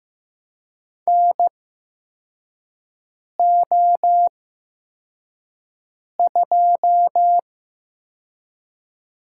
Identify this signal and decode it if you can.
Morse code, heard as a keyed tone, about 15 wpm, spaced as for 4 wpm, 700 Hz. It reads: NO2
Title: USA Call Signs - 4 Characters - 8x Character Spacing - 15wpm